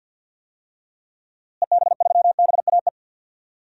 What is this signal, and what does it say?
Morse code, heard as a keyed tone, about 50 wpm, 700 Hz. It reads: EB3BAE
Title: DXCC-100 Call Signs - 6 Characters - Repeated 3x - 50wpm